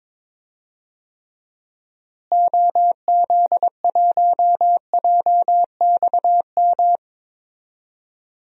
Morse code, heard as a keyed tone, about 22 wpm, 700 Hz. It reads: OZ1JXM
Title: DXCC-100 Call Signs - 6 Characters - Repeated 2x - 22wpm